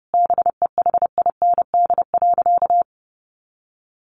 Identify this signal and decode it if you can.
Morse code, heard as a keyed tone, about 30 wpm, 700 Hz. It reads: BEHIND.
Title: A Sound Of Thunder 30wpm